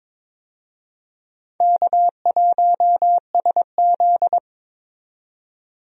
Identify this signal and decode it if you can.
Morse code, heard as a keyed tone, about 21 wpm, 700 Hz. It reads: K1SZ
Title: USA Call Signs - 4 Characters - 22wpm